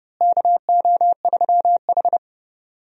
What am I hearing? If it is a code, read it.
Morse code, heard as a keyed tone, about 30 wpm, 700 Hz. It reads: KO3H
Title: USA Call Signs - 4 Characters - Repeated 3x - 30wpm